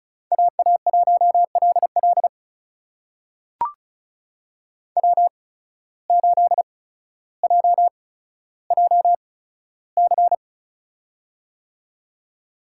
Morse code, heard as a keyed tone, about 36 wpm, spaced as for 15 wpm, 700 Hz. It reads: AA1LL W8JJC
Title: USA Call Signs - 5 Characters - 8x Character Spacing - 35wpm